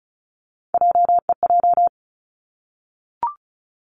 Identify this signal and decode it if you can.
Morse code, heard as a keyed tone, about 36 wpm, 700 Hz. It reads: JEJ
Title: Three Challenging Letter-Number Combinations 35wpm